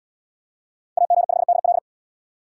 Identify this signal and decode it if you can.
Morse code, heard as a keyed tone, about 71 wpm, 700 Hz. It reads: AG5DL